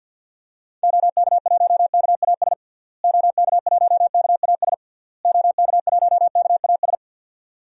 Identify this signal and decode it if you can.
Morse code, heard as a keyed tone, about 52 wpm, 700 Hz. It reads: OK1KAS OK1KAS OK1KAS
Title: DXCC-100 Call Signs - 6 Characters - Repeated 3x - 50wpm